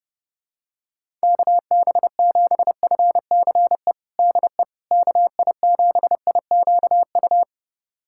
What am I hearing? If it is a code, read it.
Morse code, heard as a keyed tone, about 28 wpm, 700 Hz. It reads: KB7FCE DE KI7IQU
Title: USA Callsign DE Callsign - 6 Characters - 30wpm